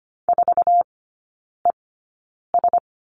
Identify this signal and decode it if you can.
Morse code, heard as a keyed tone, about 25 wpm, spaced as for 9 wpm, 700 Hz. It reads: 4ES
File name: Lesson 15e - Number 5 - Three Characters 25wpm